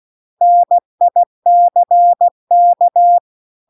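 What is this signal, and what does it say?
Morse code, heard as a keyed tone, about 16 wpm, 700 Hz. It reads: NICK
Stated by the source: Morse practice text at 16 WPM.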